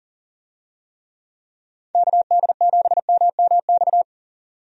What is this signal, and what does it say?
Morse code, heard as a keyed tone, about 40 wpm, 700 Hz. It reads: KD7MMX
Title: USA Call Signs - 6 Characters - 40wpm